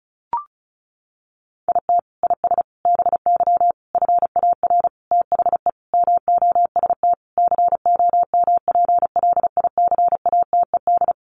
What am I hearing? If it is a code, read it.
Morse code, heard as a keyed tone, about 34 wpm, 700 Hz. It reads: IT IS BY FAR THE MOST COMPLICATED
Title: Sentences from Top 1200 Words 35wpm